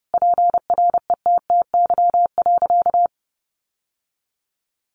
Morse code, heard as a keyed tone, about 30 wpm, 700 Hz. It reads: PRETTY.
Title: Sentences from Top 1000 Words 30wpm